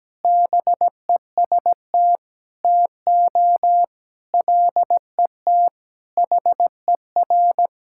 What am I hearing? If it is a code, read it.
Morse code, heard as a keyed tone, about 17 wpm, 700 Hz. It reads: BEST TO LET HER
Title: Sentences from Top 1000 Words - Review Part 6 - 17wpm